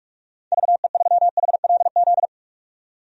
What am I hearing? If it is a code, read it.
Morse code, heard as a keyed tone, about 41 wpm, 700 Hz. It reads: VE3HLZ
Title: DXCC-100 Call Signs - 6 Characters - 45wpm